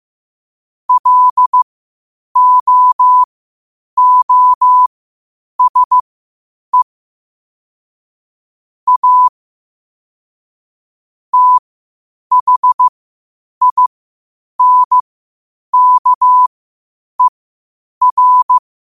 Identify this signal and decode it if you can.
Morse code, heard as a keyed tone, about 15 wpm, spaced as for 8 wpm, 1000 Hz. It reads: LOOSE A THINKER
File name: 8wpm